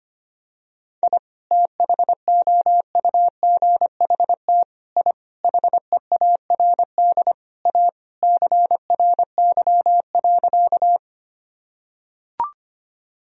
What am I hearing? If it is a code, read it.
Morse code, heard as a keyed tone, about 25 wpm, 700 Hz. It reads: I THOUGHT I HEARD A CRY.
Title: Sentences from Top 1000 Words - Review Part 2 - 25wpm